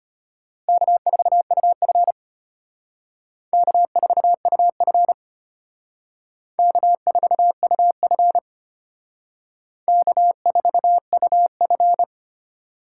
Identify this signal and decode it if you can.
Morse code, heard as a keyed tone, about 33 wpm, 700 Hz. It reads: K4UF K4UF K4UF K4UF